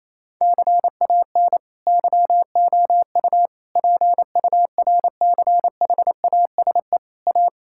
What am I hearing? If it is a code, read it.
Morse code, heard as a keyed tone, about 28 wpm, 700 Hz. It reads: CAN YOU PURCHASE A
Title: Sentences from Top 1000 Words - Review Part 5 - 28wpm